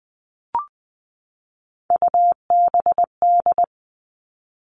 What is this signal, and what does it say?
Morse code, heard as a keyed tone, about 20 wpm, 700 Hz. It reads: UBD
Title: All Cut Numbers - 3-Digit 20wpm